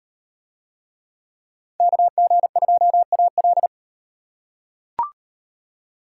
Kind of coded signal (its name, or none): Morse code